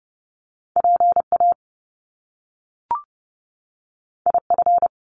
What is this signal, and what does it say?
Morse code, heard as a keyed tone, about 30 wpm, 700 Hz. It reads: P A I F
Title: Top Two Letter Combinations in Words 30wpm